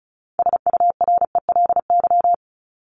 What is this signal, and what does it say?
Morse code, heard as a keyed tone, about 34 wpm, 700 Hz. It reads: SURELY
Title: LY Suffix - Top 100 Words with Base Words 35wpm